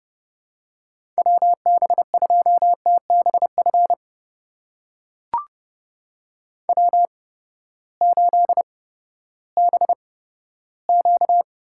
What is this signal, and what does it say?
Morse code, heard as a keyed tone, about 30 wpm, spaced as for 15 wpm, 700 Hz. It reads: WB2TBF W8BQ
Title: USA Call Signs - 8x Character Spacing - 30wpm